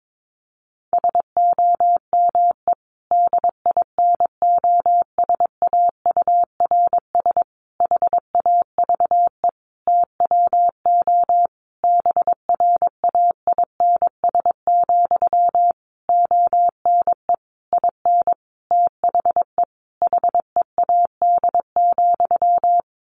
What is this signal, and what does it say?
Morse code, heard as a keyed tone, about 22 wpm, 700 Hz. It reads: SOME DINOSAURS HAVE TWO BRAINS, ONE IN THE HEAD,
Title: A Sound Of Thunder 22wpm